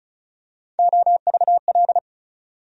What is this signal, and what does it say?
Morse code, heard as a keyed tone, about 34 wpm, 700 Hz. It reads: OVL
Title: USA Call Sign Postfixes - 3 Letters 35wpm